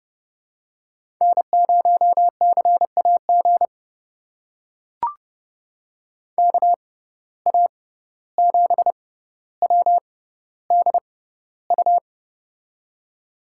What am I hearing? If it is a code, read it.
Morse code, heard as a keyed tone, about 32 wpm, spaced as for 15 wpm, 700 Hz. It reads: N0CAG KA7WDU
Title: USA Call Signs - 6x Character Spacing - 30wpm